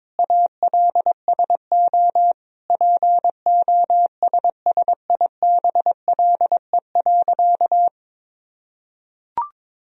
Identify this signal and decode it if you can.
Morse code, heard as a keyed tone, about 22 wpm, 700 Hz. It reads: ALSO POSSIBLE.
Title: Sentences from Top 1000 Words - Review Part 1 - 22wpm